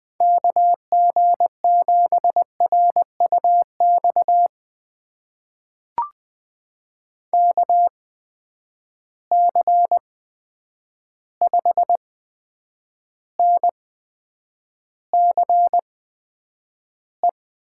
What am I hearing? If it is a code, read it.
Morse code, heard as a keyed tone, about 20 wpm, spaced as for 9 wpm, 700 Hz. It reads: KG7RUX KC5NCE
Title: USA Call Signs - 6 Characters - 8x Character Spacing - 20wpm